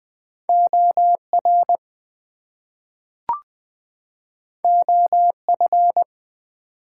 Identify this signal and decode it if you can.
Morse code, heard as a keyed tone, about 20 wpm, 700 Hz. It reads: OR OF
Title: Two Letter Words - 20wpm